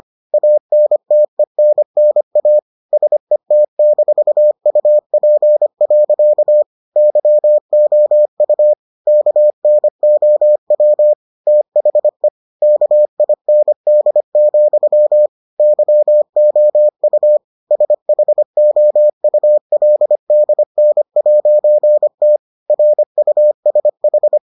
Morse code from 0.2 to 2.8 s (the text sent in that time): ANTENNA